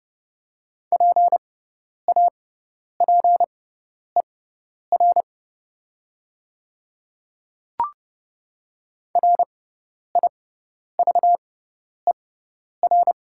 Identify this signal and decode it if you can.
Morse code, heard as a keyed tone, about 30 wpm, spaced as for 10 wpm, 700 Hz. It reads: PAPER RIVER
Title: 6x Character Spacing - Top 300 Words - 30wpm